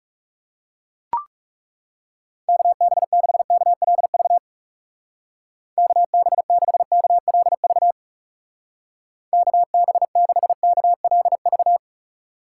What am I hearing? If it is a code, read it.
Morse code, heard as a keyed tone, about 37 wpm, 700 Hz. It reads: KB6KLV KB6KLV KB6KLV
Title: Speed Racing - USA Call Signs 30wpm